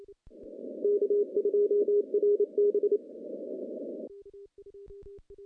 Morse code, heard as a keyed tone, about 28 wpm, 410 Hz. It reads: K2RB
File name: contests / CW-SS-k5tr- -0400